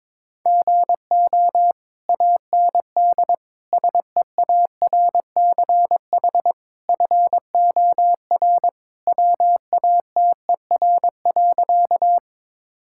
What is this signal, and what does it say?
Morse code, heard as a keyed tone, about 22 wpm, 700 Hz. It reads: GO AND SEARCH FOR WATER.
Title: Sentences from Top 1000 Words - Review Part 5 - 22wpm